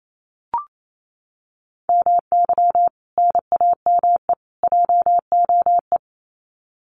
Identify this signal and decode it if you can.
Morse code, heard as a keyed tone, about 27 wpm, 700 Hz. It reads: MY NAME JOE
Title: Common QSO Phrases 28wpm